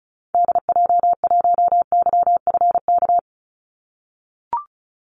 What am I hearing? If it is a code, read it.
Morse code, heard as a keyed tone, about 34 wpm, 700 Hz. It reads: DJ1YFK
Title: Call Signs - 35wpm